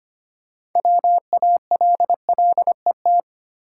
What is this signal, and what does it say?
Morse code, heard as a keyed tone, about 25 wpm, 700 Hz. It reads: WALLET
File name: practice / 25wpm / Easy Words with Hints - 25wpm